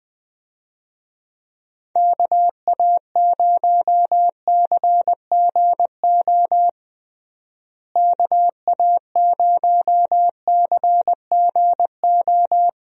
Morse code, heard as a keyed tone, about 20 wpm, 700 Hz. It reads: KA0CGO KA0CGO
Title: USA Call Signs - 6 Characters - Repeated 3x - 20wpm